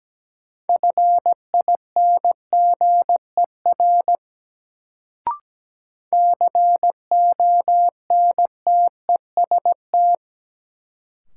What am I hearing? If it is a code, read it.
Morse code, heard as a keyed tone, about 17 wpm, 700 Hz. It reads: FINGER CONTEST
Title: Sentences from Top 1600 Words - New Word Drill - 17wpm